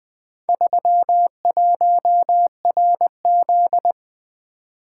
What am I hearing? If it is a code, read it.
Morse code, heard as a keyed tone, about 20 wpm, 700 Hz. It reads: 31RZ
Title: Four Letter-Number Combinations - 20wpm